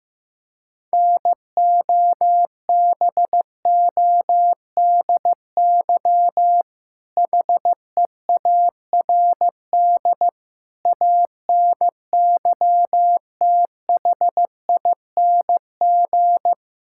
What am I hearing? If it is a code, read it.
Morse code, heard as a keyed tone, about 15 wpm, 700 Hz. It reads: NOBODY HEARD ANYTHING
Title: Sentences from Top 1600 Words 15wpm